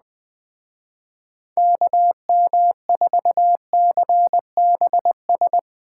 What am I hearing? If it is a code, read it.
Morse code, heard as a keyed tone, about 20 wpm, 700 Hz. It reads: KM4CBS
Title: Speed Racing - USA 2x3 Call Signs 17wpm